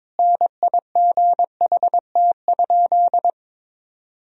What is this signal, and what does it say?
Morse code, heard as a keyed tone, about 22 wpm, 700 Hz. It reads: NIGHT?
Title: Sentences from Top 1000 Words - Review Part 4 - 22wpm